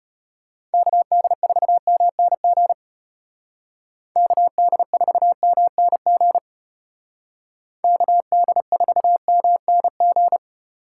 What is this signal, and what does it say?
Morse code, heard as a keyed tone, about 34 wpm, 700 Hz. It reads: KD4MNG KD4MNG KD4MNG